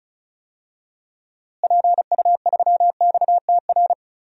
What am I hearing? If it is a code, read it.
Morse code, heard as a keyed tone, about 36 wpm, 700 Hz. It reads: PU3XTR
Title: DXCC-100 Call Signs - 6 Characters - Repeated 2x - 35wpm